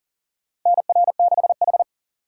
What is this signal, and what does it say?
Morse code, heard as a keyed tone, about 39 wpm, 700 Hz. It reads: NR6H
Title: USA Call Signs - 4 Characters - 2x Character Spacing - 40wpm